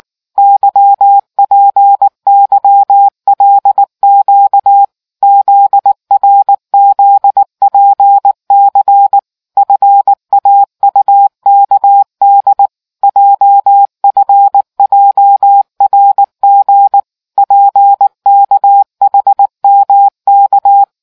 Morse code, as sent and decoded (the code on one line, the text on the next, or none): -.-- .--. -.-- .-.. --.- / --.. .-. --.. .--. -.-. / ..-. .- ..- -.- -.. / .--- ..-. .--- .-. --. / .--. -.- .... -- -.-
YPYLQ ZRZPC FAUKD JFJRG PKHMK